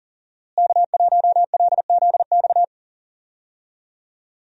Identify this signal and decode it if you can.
Morse code, heard as a keyed tone, about 40 wpm, 700 Hz. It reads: K1LZX
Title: USA Call Signs - 5 Characters - Repeated 2x - 40wpm